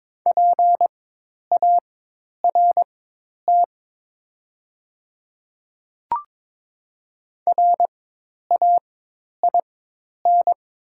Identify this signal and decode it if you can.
Morse code, heard as a keyed tone, about 22 wpm, spaced as for 10 wpm, 690 Hz. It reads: PART RAIN
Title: 4x Character Spacing - Top 500 Words - 22wpm